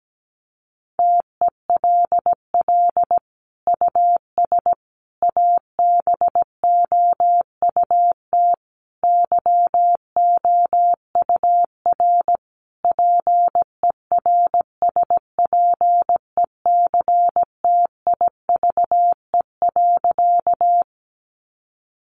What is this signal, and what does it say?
Morse code, heard as a keyed tone, about 17 wpm, 700 Hz. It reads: TELL US ABOUT YOUR PERSPECTIVE.
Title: Sentences from Top 1200 Words 17wpm